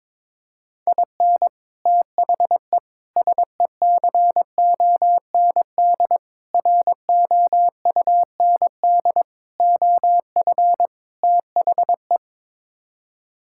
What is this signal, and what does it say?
Morse code, heard as a keyed tone, about 22 wpm, 700 Hz. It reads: IN THE SECOND ROUND OF THE
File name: practice / 22wpm / Sets of 6 Words - 22wpm